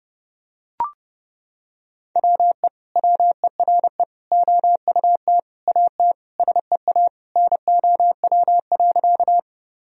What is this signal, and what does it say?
Morse code, heard as a keyed tone, about 32 wpm, 700 Hz. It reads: WE WERE OUT AT SEA NOW.
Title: Sentences from Top 1000 Words - Review Part 1 - 30wpm